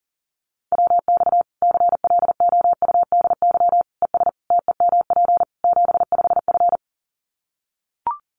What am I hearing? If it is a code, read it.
Morse code, heard as a keyed tone, about 40 wpm, 700 Hz. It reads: WX CLOUDY ES TEMP 75F
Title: Common QSO Phrases 40wpm